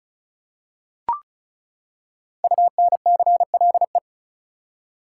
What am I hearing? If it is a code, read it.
Morse code, heard as a keyed tone, about 36 wpm, 700 Hz. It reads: UNCLE
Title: Five Letter Words - 35wpm